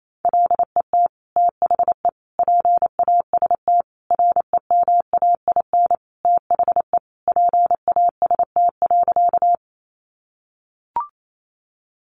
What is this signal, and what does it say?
Morse code, heard as a keyed tone, about 28 wpm, 700 Hz. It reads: LET THE PAST REMAIN THE PAST.
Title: Sentences from Top 1200 Words 28wpm